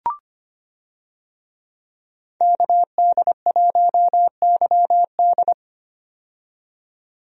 Morse code, heard as a keyed tone, about 25 wpm, 700 Hz. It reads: KD1YD
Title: USA Call Signs - 5 Characters - 25wpm